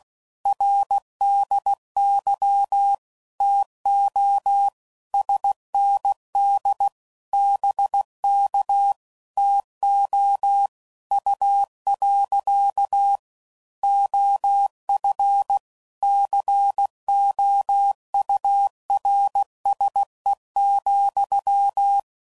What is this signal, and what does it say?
Morse code, heard as a keyed tone, about 16 wpm, 800 Hz. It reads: RDY TO SND BK TO U. OF COURSE,
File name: CW Code Groups @15WPM / QSOs / WA7QOX